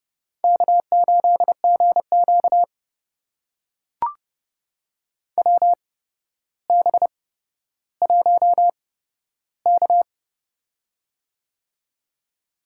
Morse code, heard as a keyed tone, about 30 wpm, spaced as for 13 wpm, 700 Hz. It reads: K8GQ WB1K
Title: USA Call Signs - 4 Characters - 8x Character Spacing - 30wpm